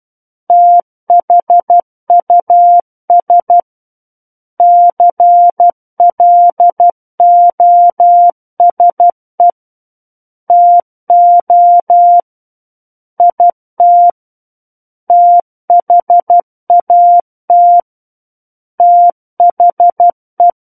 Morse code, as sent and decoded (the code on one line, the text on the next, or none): - .... ..- ... / -.-. .-.. --- ... / - --- / .. - / - .... .- - / - .... .
THUS CLOSE TO IT THAT THE